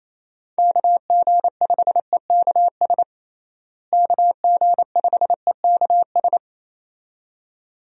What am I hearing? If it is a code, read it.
Morse code, heard as a keyed tone, about 28 wpm, 700 Hz. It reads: KG5EKS KG5EKS
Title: USA Call Signs - 6 Characters - Repeated 2x - 28wpm